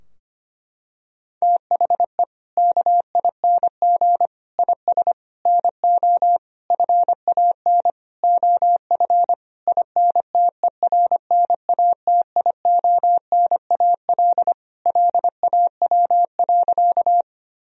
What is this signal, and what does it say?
Morse code, heard as a keyed tone, about 25 wpm, 700 Hz. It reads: THE KING IS NO FAN OF INTERNATIONAL LAW.
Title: Sentences from Top 1000 Words - Review Part 4 - 25wpm